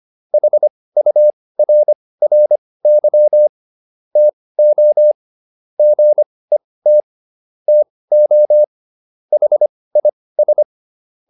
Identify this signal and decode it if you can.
Morse code, heard as a keyed tone, about 25 wpm, spaced as for 18 wpm, 600 Hz. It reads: HURRY TO GET TO HIS